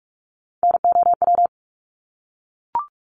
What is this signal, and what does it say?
Morse code, heard as a keyed tone, about 44 wpm, 700 Hz. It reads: NOW
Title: Top 300 Words Repeated 45wpm